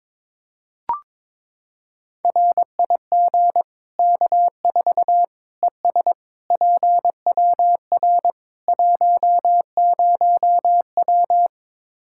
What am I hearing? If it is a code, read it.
Morse code, heard as a keyed tone, about 21 wpm, 700 Hz. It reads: RIG K4 ES PWR 10W
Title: Practice QSO - Exchange 2 - Info - 22wpm